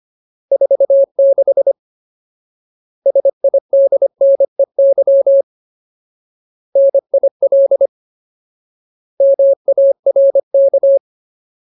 Morse code, heard as a keyed tone, about 25 wpm, 550 Hz. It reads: 46 SIDNEY NIL MARK
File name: Ultimate Word List / 25WPM Ultimate